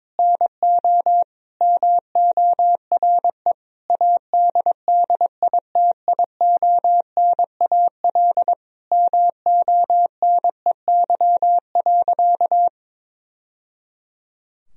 Morse code, heard as a keyed tone, about 22 wpm, 700 Hz. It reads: NO MORE ADDITIONAL MONEY.
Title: Sentences from Top 1000 Words - Review Part 4 - 22wpm